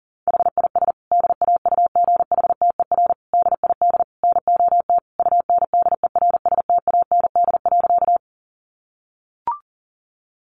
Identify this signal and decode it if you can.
Morse code, heard as a keyed tone, about 40 wpm, 700 Hz. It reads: HIS DAUGHTER DID NOT UNDERSTAND.
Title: Sentences from Top 1500 Words 40wpm